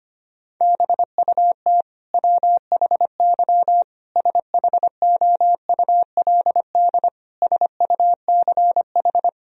Morse code, heard as a keyed tone, about 25 wpm, 700 Hz. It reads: BUT WHY SHOULD SUCH
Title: Sentences from Top 1000 Words - Review Part 1 - 25wpm